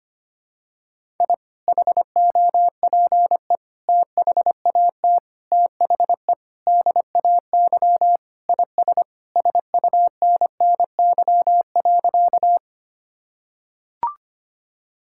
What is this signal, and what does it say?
Morse code, heard as a keyed tone, about 25 wpm, 700 Hz. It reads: I HOPE THAT THE DAY IS SUNNY.
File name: practice / 25wpm / Sentences with 2-Word Phrases - 25wpm